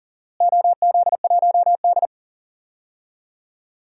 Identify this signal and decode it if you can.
Morse code, heard as a keyed tone, about 40 wpm, 700 Hz. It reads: OZ1D
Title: DXCC-100 Call Signs - 4 Characters - Repeated 2x - 40wpm